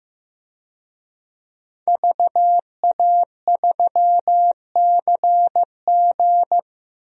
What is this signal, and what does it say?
Morse code, heard as a keyed tone, about 15 wpm, 700 Hz. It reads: VA3CG
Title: DXCC-100 Call Signs - 5 Characters - Repeated 2x - 15wpm